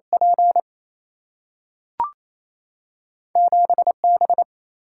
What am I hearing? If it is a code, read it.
Morse code, heard as a keyed tone, about 28 wpm, 700 Hz. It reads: P 7B
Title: Single-Letter Single-Number 28wpm